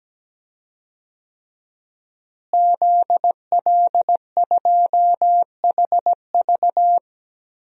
Morse code, heard as a keyed tone, about 17 wpm, 700 Hz. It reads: ZL2HV
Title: DXCC-100 Call Signs - 5 Characters - Repeated 3x - 17wpm